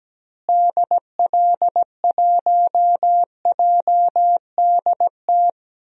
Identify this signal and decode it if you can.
Morse code, heard as a keyed tone, about 17 wpm, 700 Hz. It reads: DL1JDT
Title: DXCC-100 Call Signs - 6 Characters - Repeated 3x - 17wpm